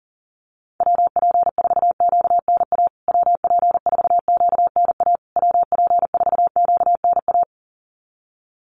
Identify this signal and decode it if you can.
Morse code, heard as a keyed tone, about 40 wpm, 700 Hz. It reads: WP4QNA WP4QNA WP4QNA
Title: USA Call Signs Repeated 3x 40wpm